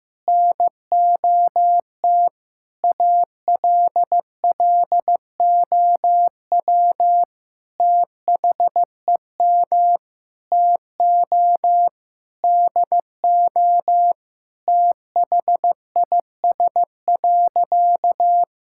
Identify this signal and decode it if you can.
Morse code, heard as a keyed tone, about 15 wpm, 700 Hz. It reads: NOT ALLOW THEM TO DO THIS.
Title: Sentences from Top 1000 Words - Review Part 5 - 15wpm